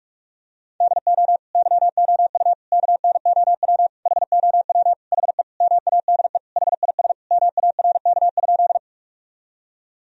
Morse code, heard as a keyed tone, about 44 wpm, 700 Hz. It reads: DO YOU KNOW HOW HE MADE HIS MARK?
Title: Sentences from Top 1000 Words - Review Part 1 - 45wpm